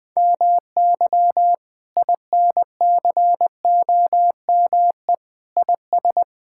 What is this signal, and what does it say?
Morse code, heard as a keyed tone, about 20 wpm, 700 Hz. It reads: MY INCOME IS
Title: Sentences from Top 1000 Words - Review Part 5 - 20wpm